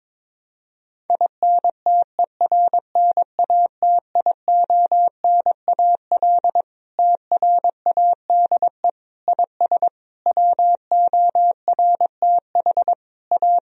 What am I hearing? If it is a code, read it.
Morse code, heard as a keyed tone, about 22 wpm, 700 Hz. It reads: INTERNATIONAL TRADE IS WORTH A
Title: Sentences from Top 1000 Words - Review Part 4 - 22wpm